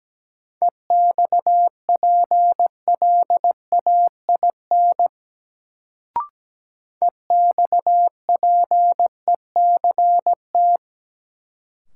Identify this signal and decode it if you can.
Morse code, heard as a keyed tone, about 17 wpm, 700 Hz. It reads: EXPLAIN EXPECT
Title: Sentences from Top 1100 Words - New Word Drill - 17wpm